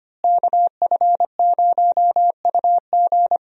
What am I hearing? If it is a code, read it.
Morse code, heard as a keyed tone, about 25 wpm, 700 Hz. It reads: KF0UG
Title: USA Call Signs - 5 Characters - Repeated 3x - 25wpm